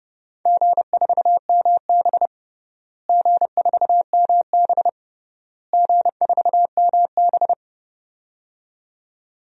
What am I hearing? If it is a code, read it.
Morse code, heard as a keyed tone, about 30 wpm, 700 Hz. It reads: G4MB G4MB G4MB
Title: DXCC-100 Call Signs - 4 Characters - Repeated 3x - 30wpm